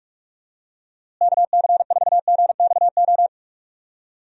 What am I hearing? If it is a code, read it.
Morse code, heard as a keyed tone, about 42 wpm, 700 Hz. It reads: KC4GXO
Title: USA Call Signs - 6 Characters - Repeated 3x - 45wpm